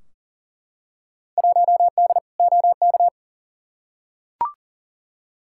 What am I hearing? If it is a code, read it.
Morse code, heard as a keyed tone, about 40 wpm, 700 Hz. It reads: 1D OK
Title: Field Day Exchanges - 40wpm